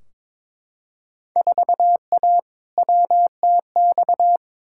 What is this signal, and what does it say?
Morse code, heard as a keyed tone, about 22 wpm, 700 Hz. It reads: 4A WTX
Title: Field Day Exchanges Encore - 22wpm